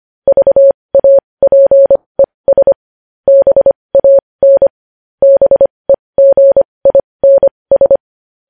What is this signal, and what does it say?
Morse code, heard as a keyed tone, about 25 wpm, spaced as for 20 wpm, 550 Hz. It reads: VAPES BAN BEGINS
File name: podcast / News headlines-1x-25x20